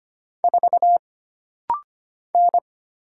Morse code, heard as a keyed tone, about 25 wpm, 700 Hz. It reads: 4 N
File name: Single Character (Letter:Number:Punctuation:Prosign) - Rapid-Fire - 25wpm